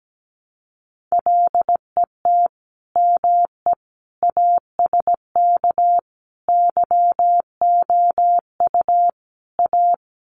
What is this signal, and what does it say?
Morse code, heard as a keyed tone, about 17 wpm, 700 Hz. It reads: LET ME ASK YOU A
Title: Sets of 5 Words Encore 17wpm